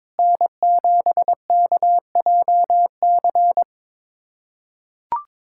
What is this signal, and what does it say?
Morse code, heard as a keyed tone, about 22 wpm, 700 Hz. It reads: N7KJC
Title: USA Callsign DE Callsign - 5 Characters - 22wpm